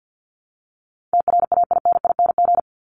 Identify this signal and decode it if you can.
Morse code, heard as a keyed tone, about 50 wpm, 700 Hz. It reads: TRAINING
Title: Top 501-750 Words 50wpm